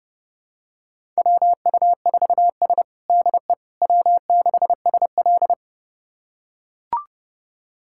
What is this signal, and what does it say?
Morse code, heard as a keyed tone, about 30 wpm, 700 Hz. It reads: WU4S DE W6SL
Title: USA Callsign DE Callsign - 4 Characters - 30wpm